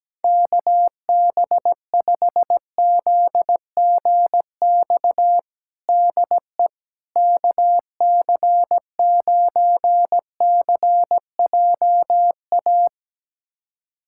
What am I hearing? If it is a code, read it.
Morse code, heard as a keyed tone, about 17 wpm, 700 Hz. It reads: KB5ZGX DE KC9CJA
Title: USA Callsign DE Callsign - 6 Characters - 17wpm